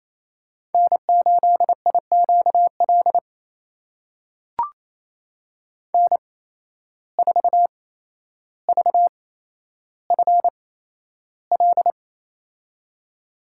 Morse code, heard as a keyed tone, about 28 wpm, spaced as for 12 wpm, 700 Hz. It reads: N8IQL N4VFL
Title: USA Call Signs - 5 Characters - 8x Character Spacing - 28wpm